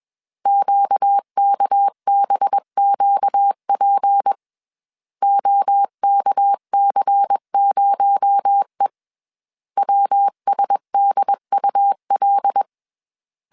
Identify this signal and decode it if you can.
Morse code, heard as a keyed tone, about 21 wpm, 800 Hz. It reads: QKBQP OKC0E WSDUL
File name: CW / DET / 1 6 1500C S C 20wpm 2116